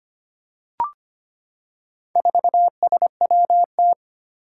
Morse code, heard as a keyed tone, about 25 wpm, 700 Hz. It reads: 4SWT
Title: Four Challenging Letter-Number Combinations 25wpm